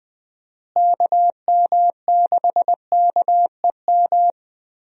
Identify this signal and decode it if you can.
Morse code, heard as a keyed tone, about 20 wpm, 700 Hz. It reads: KM6KEM
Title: USA Call Signs - 6 Characters - Repeated 2x - 20wpm